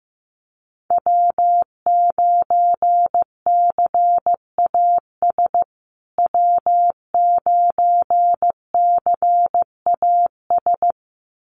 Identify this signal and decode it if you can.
Morse code, heard as a keyed tone, about 15 wpm, 700 Hz. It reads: W9CAS W9CAS
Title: USA Call Signs Repeated 2x 15wpm